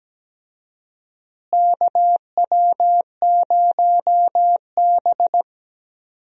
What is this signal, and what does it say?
Morse code, heard as a keyed tone, about 17 wpm, 700 Hz. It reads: KW0B
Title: USA Call Signs - 4 Characters - Repeated 3x - 17wpm